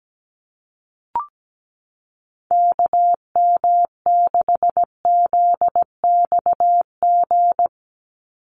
Morse code, heard as a keyed tone, about 17 wpm, 700 Hz. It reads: KM6ZXG
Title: USA Call Signs 17wpm